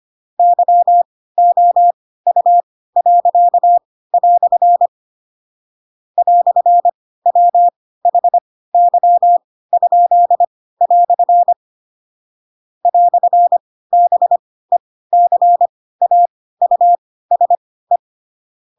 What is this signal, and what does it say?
Morse code, heard as a keyed tone, about 25 wpm, spaced as for 16 wpm, 700 Hz. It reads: YOU." "WHY?" "BECAUSE
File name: Pinocchio / Pinocchio ch4 10 25 ik1hge